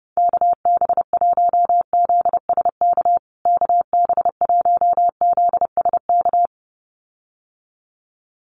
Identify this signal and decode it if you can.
Morse code, heard as a keyed tone, about 30 wpm, 700 Hz. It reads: KB1ZSK KB1ZSK
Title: USA Call Signs Repeated 2x 30wpm